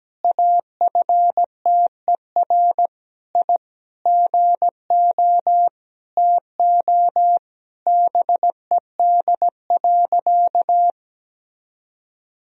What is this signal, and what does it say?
Morse code, heard as a keyed tone, about 17 wpm, 700 Hz. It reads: AFTER I GO TO BED.
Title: Sentences from Top 1000 Words - Review Part 2 - 17wpm